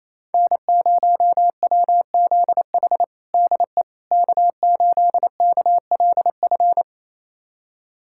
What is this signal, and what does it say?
Morse code, heard as a keyed tone, about 27 wpm, 700 Hz. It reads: N0WZH DE K8KLF
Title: USA Callsign DE Callsign - 5 Characters - 28wpm